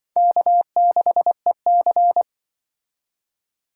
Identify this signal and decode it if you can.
Morse code, heard as a keyed tone, about 24 wpm, 700 Hz. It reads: K6EC
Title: Speed Racing - USA Call Signs 20wpm